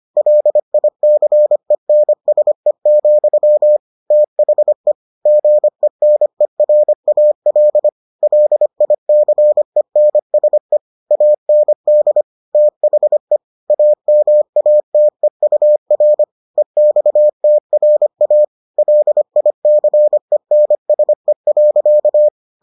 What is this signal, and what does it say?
Morse code, heard as a keyed tone, about 25 wpm, 600 Hz. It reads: LICENSE, THE GENERAL LICENSE AND THE AMATEUR EXTRA LICENSE.